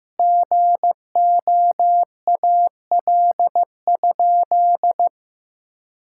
Morse code, heard as a keyed tone, about 16 wpm, 700 Hz. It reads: GOAL?
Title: Sentences from Top 1000 Words - Review Part 6 - 15wpm